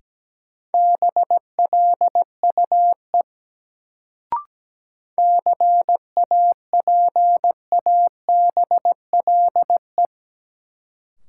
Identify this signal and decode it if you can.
Morse code, heard as a keyed tone, about 17 wpm, 700 Hz. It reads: BLUE CAPABLE
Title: Sentences from Top 1200 Words - New Word Drill Alt - 17wpm